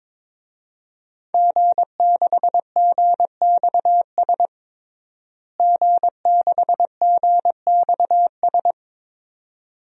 Morse code, heard as a keyed tone, about 22 wpm, 700 Hz. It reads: G6GXS G6GXS
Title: DXCC-100 Call Signs - 5 Characters - Repeated 2x - 22wpm